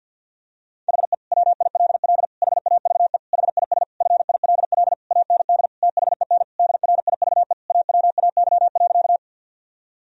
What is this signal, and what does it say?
Morse code, heard as a keyed tone, about 51 wpm, 700 Hz. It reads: HE WILL HAVE HIS FILL AND THEN DRIVE AWAY.
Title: Sentences from Top 1000 Words - Review Part 3 - 50wpm